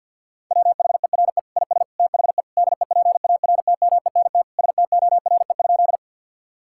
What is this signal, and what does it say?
Morse code, heard as a keyed tone, about 50 wpm, 700 Hz. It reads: WHERE IS THE DEPARTMENT STORE?
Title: Sentences from Top 1000 Words - Review Part 5 - 50wpm